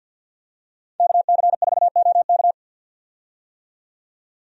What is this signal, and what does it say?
Morse code, heard as a keyed tone, about 51 wpm, 700 Hz. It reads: KC4OK
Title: USA Call Signs - 5 Characters - 50wpm